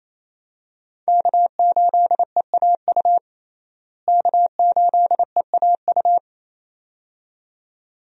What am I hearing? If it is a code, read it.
Morse code, heard as a keyed tone, about 28 wpm, 700 Hz. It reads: K8EAU K8EAU
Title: USA Call Signs - 5 Characters - Repeated 2x - 28wpm